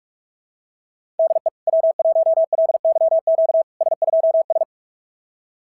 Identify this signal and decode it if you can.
Morse code, heard as a keyed tone, about 46 wpm, 690 Hz. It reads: DE W1LYQ S2S
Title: SOTA QSO Practice 45wpm